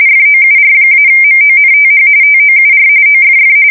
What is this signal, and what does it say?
RTTY, about 45.45 baud. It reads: FOUR-MONTH TOUR OFF
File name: fsk170 / battleshipnj